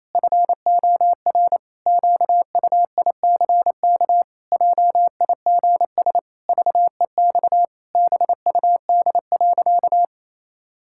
Morse code, heard as a keyed tone, about 28 wpm, 700 Hz. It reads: FOR QUICK JIGS VEX BUD.
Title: Pangram Challenge v2 - 28wpm